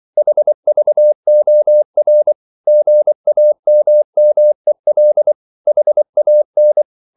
Morse code, heard as a keyed tone, about 24 wpm, 600 Hz. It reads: HVOR GAMMEL HAN
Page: Ferie 23 wpm